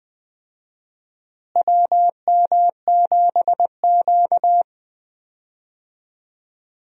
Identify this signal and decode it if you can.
Morse code, heard as a keyed tone, about 20 wpm, 700 Hz. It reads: WM7Q
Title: USA Call Signs - 4 Characters - Repeated 3x - 20wpm